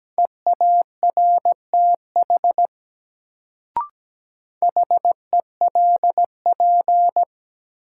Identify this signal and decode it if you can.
Morse code, heard as a keyed tone, about 17 wpm, 700 Hz. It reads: EARTH HELP
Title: Sentences from Top 200 Words - New Word Drill - 17wpm